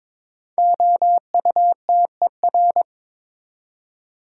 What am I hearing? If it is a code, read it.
Morse code, heard as a keyed tone, about 22 wpm, 700 Hz. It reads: OUTER
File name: practice / 22wpm / 5 Letter Words - First 3 Letters Given - 22wpm